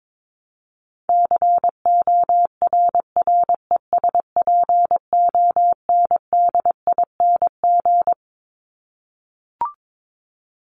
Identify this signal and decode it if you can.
Morse code, heard as a keyed tone, about 22 wpm, 700 Hz. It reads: CORRESPONDING
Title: ING Suffix - Top 100 Words with Base Words 22wpm